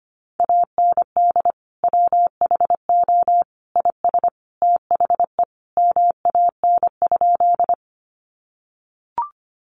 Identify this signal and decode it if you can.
Morse code, heard as a keyed tone, about 25 wpm, 700 Hz. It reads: AND WHO IS THE MAN?
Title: Sentences from Top 200 Words 25wpm